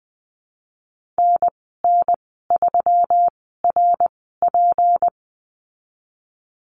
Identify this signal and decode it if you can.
Morse code, heard as a keyed tone, about 20 wpm, spaced as for 15 wpm, 700 Hz. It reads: NN3RP
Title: 154 Lesson 25g - Letter p - Call Signs 20wpm